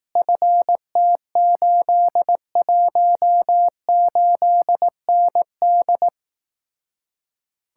Morse code, heard as a keyed tone, about 18 wpm, 700 Hz. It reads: FT818ND
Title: Speed-Racing Common QSO Phrases Encore - 15wpm